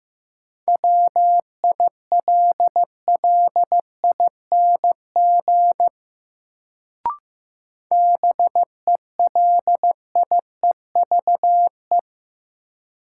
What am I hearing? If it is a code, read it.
Morse code, heard as a keyed tone, about 15 wpm, 700 Hz. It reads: WILLING BELIEVE
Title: Sentences from Top 1000 Words - New Word Drill - 15wpm